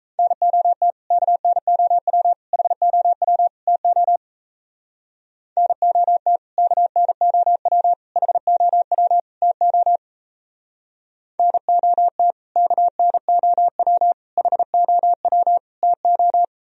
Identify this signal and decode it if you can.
Morse code, heard as a keyed tone, about 35 wpm, 700 Hz. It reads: NOT KNOW HOW TO NOT KNOW HOW TO NOT KNOW HOW TO